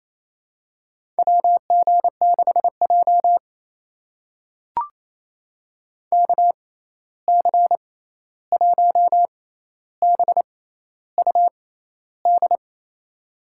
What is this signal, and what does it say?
Morse code, heard as a keyed tone, about 28 wpm, spaced as for 13 wpm, 700 Hz. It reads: WG6J KC1BUD
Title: USA Call Signs - 6x Character Spacing - 28wpm